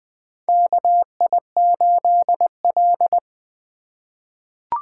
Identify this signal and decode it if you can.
Morse code, heard as a keyed tone, about 20 wpm, 700 Hz. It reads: KI8L
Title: USA Call Signs - 4 Characters - 3x Character Spacing - 20wpm